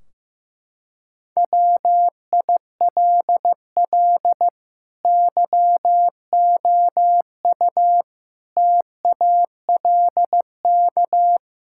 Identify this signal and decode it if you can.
Morse code, heard as a keyed tone, about 16 wpm, 700 Hz. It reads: WILL YOU TALK
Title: Sentences from Top 1600 Words 15wpm